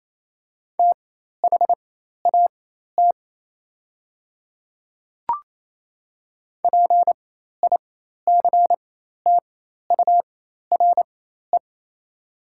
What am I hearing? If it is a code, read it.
Morse code, heard as a keyed tone, about 28 wpm, spaced as for 13 wpm, 690 Hz. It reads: THAT PICTURE
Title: 4x Character Spacing - Top 200 Words - 28wpm